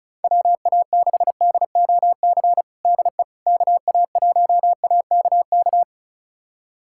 Morse code, heard as a keyed tone, about 34 wpm, 690 Hz. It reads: WA6DOC DE KA1AKK
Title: USA Callsign DE Callsign - 6 Characters - 35wpm